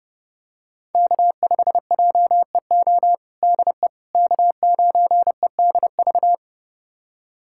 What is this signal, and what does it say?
Morse code, heard as a keyed tone, about 32 wpm, 700 Hz. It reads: K5JEO DE K9EDV
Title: USA Callsign DE Callsign - 5 Characters - 30wpm